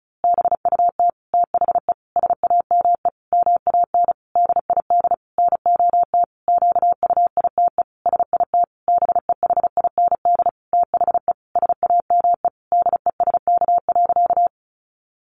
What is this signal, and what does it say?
Morse code, heard as a keyed tone, about 34 wpm, 700 Hz. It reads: BUT THE SAME MAN DID NOT QUITE SIT BEHIND THE SAME DESK.
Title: A Sound Of Thunder 35wpm